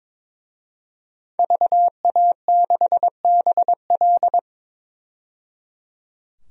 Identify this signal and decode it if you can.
Morse code, heard as a keyed tone, about 22 wpm, 700 Hz. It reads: VA6BL
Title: DXCC-100 Call Signs - 5 Characters - 22wpm